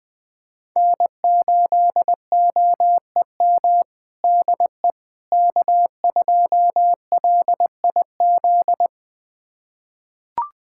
Morse code, heard as a keyed tone, about 21 wpm, 700 Hz. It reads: N8OEM DE K2LIZ
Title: USA Callsign DE Callsign 20wpm